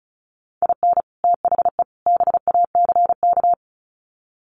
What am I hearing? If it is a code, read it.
Morse code, heard as a keyed tone, about 34 wpm, 700 Hz. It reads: IN THE BACK
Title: Sets of 3 Words 35wpm